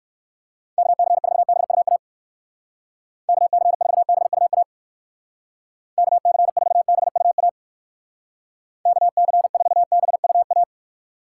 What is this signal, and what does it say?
Morse code, heard as a keyed tone, about 56 wpm, 700 Hz. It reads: KC4BUA KC4BUA KC4BUA KC4BUA